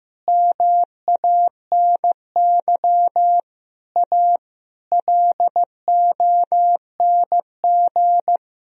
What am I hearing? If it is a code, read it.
Morse code, heard as a keyed tone, about 15 wpm, 700 Hz. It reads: MANY A LONG
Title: Sentences from Top 1000 Words - Review Part 1 - 15wpm